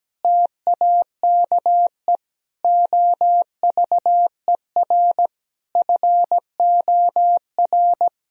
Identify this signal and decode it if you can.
Morse code, heard as a keyed tone, about 17 wpm, 700 Hz. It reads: TAKE OVER FOR
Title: Sentences from Top 1000 Words - Review Part 3 - 17wpm